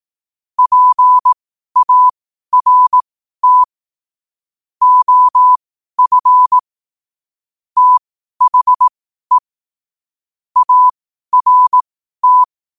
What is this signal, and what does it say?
Morse code, heard as a keyed tone, about 18 wpm, spaced as for 12 wpm, 1000 Hz. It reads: PART OF THE ART
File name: qotdfiles / 12wpm